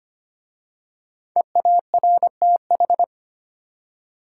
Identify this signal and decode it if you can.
Morse code, heard as a keyed tone, about 25 wpm, 700 Hz. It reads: EARTH
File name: 5 Letter Words - First 3 Letters Given - 25wpm